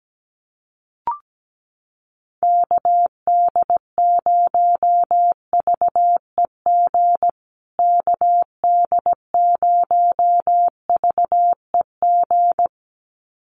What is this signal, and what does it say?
Morse code, heard as a keyed tone, about 17 wpm, 700 Hz. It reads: KD0VEG KD0VEG
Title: USA Call Signs Repeated 2x 17wpm